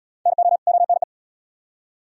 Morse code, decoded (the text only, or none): NO ONE